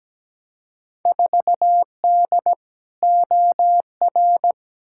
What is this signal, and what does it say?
Morse code, heard as a keyed tone, about 17 wpm, 700 Hz. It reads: 4D OR
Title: Field Day Exchanges Encore - 17wpm